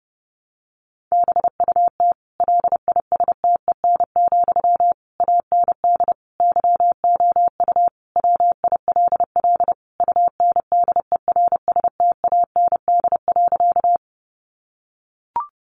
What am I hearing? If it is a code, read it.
Morse code, heard as a keyed tone, about 30 wpm, 700 Hz. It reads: BUT LISTEN, AND YOU WILL UNDERSTAND.
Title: Sentences from Top 600 Words 30wpm